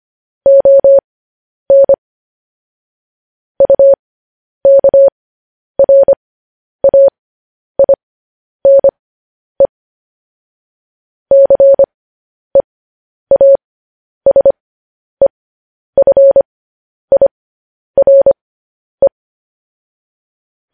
Morse code, heard as a keyed tone, about 25 wpm, spaced as for 10 wpm, 550 Hz. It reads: ON UKRAINE CEASEFIRE
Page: news headlines in morse code at 10 wpm